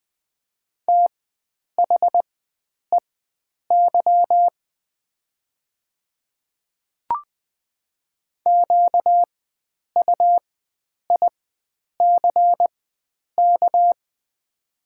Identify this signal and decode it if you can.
Morse code, heard as a keyed tone, about 20 wpm, spaced as for 9 wpm, 700 Hz. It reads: THEY QUICK
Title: 4x Character Spacing - Top 500 Words - 20wpm